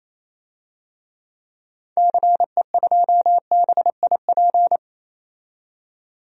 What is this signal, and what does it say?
Morse code, heard as a keyed tone, about 28 wpm, 700 Hz. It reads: CE2BIP
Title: DXCC-100 Call Signs - 6 Characters - 28wpm